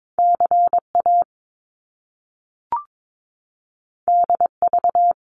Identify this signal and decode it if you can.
Morse code, heard as a keyed tone, about 22 wpm, 700 Hz. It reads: CA DV
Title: Top Two Letter Combinations in Words 22wpm